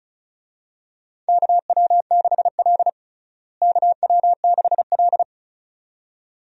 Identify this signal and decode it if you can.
Morse code, heard as a keyed tone, about 34 wpm, 700 Hz. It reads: KW6L KW6L
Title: USA Call Signs - 4 Characters - Repeated 2x - 35wpm